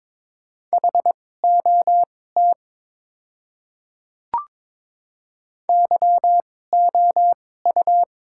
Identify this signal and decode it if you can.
Morse code, heard as a keyed tone, about 22 wpm, spaced as for 16 wpm, 700 Hz. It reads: HOT YOU
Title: 2x Character Spacing - Top 200 Words - 22wpm